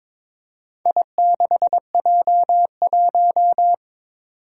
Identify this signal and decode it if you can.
Morse code, heard as a keyed tone, about 22 wpm, 700 Hz. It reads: I6J1
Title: Four Challenging Letter-Number Combinations 22wpm